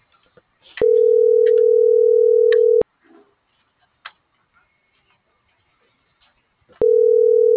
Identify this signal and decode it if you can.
Ringback tone (North America), 440 + 480 Hz.